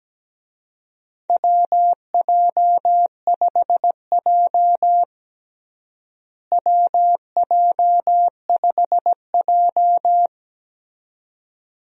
Morse code, heard as a keyed tone, about 17 wpm, 700 Hz. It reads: WJ5J WJ5J
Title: USA Call Signs - 4 Characters - Repeated 2x - 17wpm